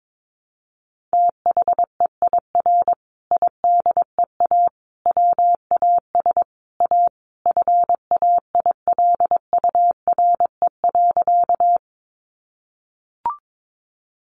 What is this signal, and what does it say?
Morse code, heard as a keyed tone, about 22 wpm, 700 Hz. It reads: THEIR IDEA WAS A FAILURE.
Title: Sentences from Top 900 Words 22wpm